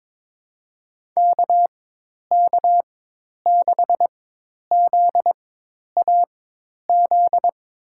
Morse code, heard as a keyed tone, about 22 wpm, spaced as for 10 wpm, 700 Hz. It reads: KK6ZAZ
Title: USA Call Signs - 4x Character Spacing - 22wpm